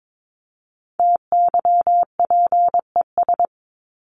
Morse code, heard as a keyed tone, about 22 wpm, 700 Hz. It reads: TYPES
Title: ES:S Suffix - Top 100 Words with Base Words 22wpm